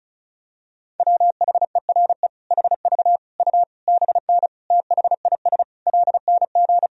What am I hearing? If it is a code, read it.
Morse code, heard as a keyed tone, about 34 wpm, 700 Hz. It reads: WHERE HV U BN THIS LNG
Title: Sentences from Top 300 Words and Abbreviations 35wpm